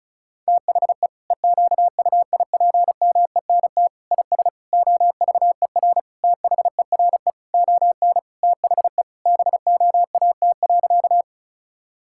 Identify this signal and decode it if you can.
Morse code, heard as a keyed tone, about 36 wpm, 700 Hz. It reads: THE EQUIPMENT IS OVER THERE ON THE BOAT.
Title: Sentences from Top 1000 Words - Review Part 5 - 35wpm